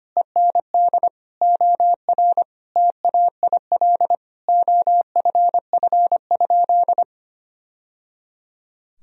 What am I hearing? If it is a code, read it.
Morse code, heard as a keyed tone, about 25 wpm, 700 Hz. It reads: END OR TAIL OFF?
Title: Sentences from Top 1000 Words - Review Part 3 - 25wpm